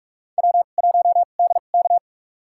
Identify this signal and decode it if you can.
Morse code, heard as a keyed tone, about 42 wpm, spaced as for 31 wpm, 700 Hz. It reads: W1DK
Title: USA Call Signs - 4 Characters - 2x Character Spacing - 45wpm